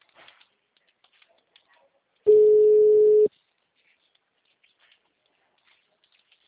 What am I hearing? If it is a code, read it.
Ringback tone (Europe, 425 Hz).